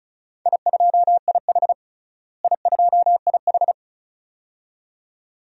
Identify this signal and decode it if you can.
Morse code, heard as a keyed tone, about 36 wpm, 700 Hz. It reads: I2IH I2IH
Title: DXCC-100 Call Signs - 4 Characters - Repeated 2x - 35wpm